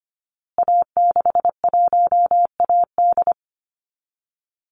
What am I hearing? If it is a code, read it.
Morse code, heard as a keyed tone, about 25 wpm, 700 Hz. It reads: A61AD
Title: Call Signs Repeated 2x - 25wpm